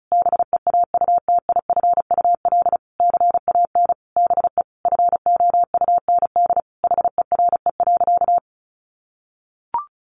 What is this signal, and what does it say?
Morse code, heard as a keyed tone, about 34 wpm, 700 Hz. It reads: BEAUTIFUL CAN BE FOUND HERE.
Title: Sentences from Top 1500 Words 35wpm